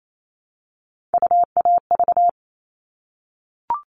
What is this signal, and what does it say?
Morse code, heard as a keyed tone, about 28 wpm, 700 Hz. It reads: UAV
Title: All Cut Numbers - 3-Digit 28wpm